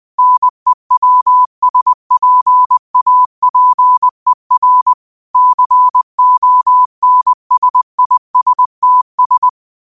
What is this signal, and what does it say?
Morse code, heard as a keyed tone, about 20 wpm, 1000 Hz. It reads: NEWSPAPER CONSISTS
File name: qotdfiles / 20wpm